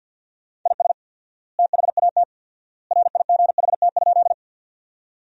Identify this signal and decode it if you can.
Morse code, heard as a keyed tone, about 48 wpm, 700 Hz. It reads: IS THAT RIGHT?
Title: 4x Word-Spacing - Sentences from Top 200 Words - 50wpm